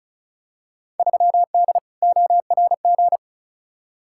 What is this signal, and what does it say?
Morse code, heard as a keyed tone, about 36 wpm, 700 Hz. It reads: 3D ORG
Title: Field Day Exchanges Encore - 35wpm